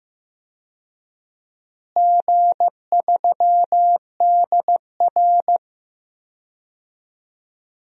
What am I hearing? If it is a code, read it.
Morse code, heard as a keyed tone, about 15 wpm, 700 Hz. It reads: G3DR
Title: DXCC-100 Call Signs - 4 Characters - Repeated 3x - 15wpm